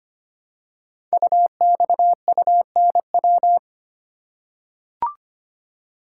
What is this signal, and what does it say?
Morse code, heard as a keyed tone, about 25 wpm, 700 Hz. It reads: UXUNW
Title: Code Groups - 5 Letters - 25wpm